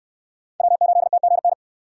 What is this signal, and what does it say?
Morse code, heard as a keyed tone, about 68 wpm, 690 Hz. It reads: W8TGN